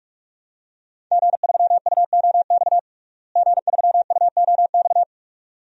Morse code, heard as a keyed tone, about 41 wpm, 700 Hz. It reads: G3UOX G3UOX
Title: DXCC-100 Call Signs - 5 Characters - Repeated 2x - 45wpm